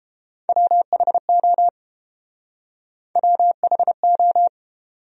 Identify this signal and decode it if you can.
Morse code, heard as a keyed tone, about 29 wpm, 700 Hz. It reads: WHO WHO